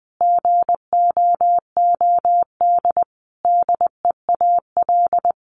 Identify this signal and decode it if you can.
Morse code, heard as a keyed tone, about 20 wpm, 700 Hz. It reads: GOOD DEAL
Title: Sentences from Top 1300 Words 20wpm